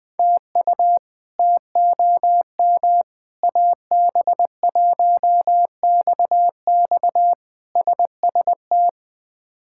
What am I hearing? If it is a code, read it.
Morse code, heard as a keyed tone, about 20 wpm, 700 Hz. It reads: TU TOM AB1XX SST
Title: K1USN SST QSO Practice - 20wpm